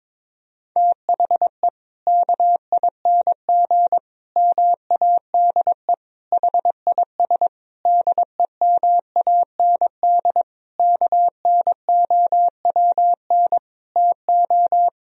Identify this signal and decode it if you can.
Morse code, heard as a keyed tone, about 22 wpm, 700 Hz. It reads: THE KING MADE HIS DEMAND KNOWN TO
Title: Sentences from Top 1000 Words - Review Part 4 - 22wpm